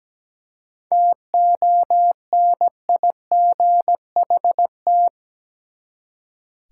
Morse code, heard as a keyed tone, about 17 wpm, 700 Hz. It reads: TONIGHT
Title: Seven Letter Words - 17wpm